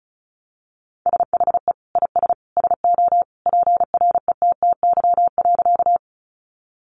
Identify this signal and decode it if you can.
Morse code, heard as a keyed tone, about 34 wpm, 700 Hz. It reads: SHE IS SO PRETTY.
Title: Sentences from Top 1000 Words 35wpm